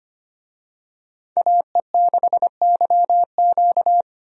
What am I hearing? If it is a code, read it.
Morse code, heard as a keyed tone, about 25 wpm, 700 Hz. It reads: AE6YQ
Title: USA Call Signs - 5 Characters - Repeated 3x - 25wpm